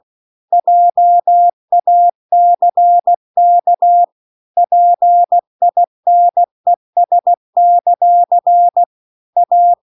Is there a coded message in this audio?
Yes: Morse code from 0.5 to 9.7 s, heard as a keyed tone, about 16 wpm, 700 Hz. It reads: JACK PINES; A